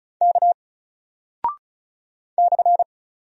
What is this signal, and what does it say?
Morse code, heard as a keyed tone, about 36 wpm, 700 Hz. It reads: K /
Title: Single Character (Letter:Number:Punctuation:Prosign) 35wpm